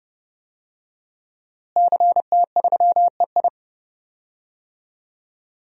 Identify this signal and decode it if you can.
Morse code, heard as a keyed tone, about 32 wpm, 700 Hz. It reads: CT3EI
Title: DXCC-100 Call Signs - 5 Characters - Repeated 2x - 30wpm